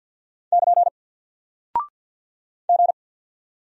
Morse code, heard as a keyed tone, about 48 wpm, 700 Hz. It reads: <KN> G
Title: Single Character (Letter:Number:Punctuation:Prosign) Alt 50wpm